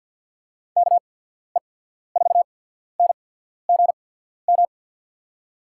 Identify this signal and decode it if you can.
Morse code, heard as a keyed tone, about 53 wpm, spaced as for 14 wpm, 700 Hz. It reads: KE4NGM
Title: USA Call Signs - 6 Characters - 8x Character Spacing - 50wpm